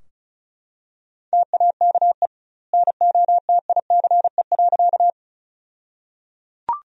Morse code, heard as a keyed tone, about 34 wpm, 700 Hz. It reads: TAKE NOTICE.
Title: 2x Word-Spacing - Sentences from Top 400 Words - 35wpm